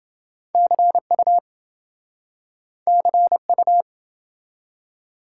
Morse code, heard as a keyed tone, about 27 wpm, 700 Hz. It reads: CU CU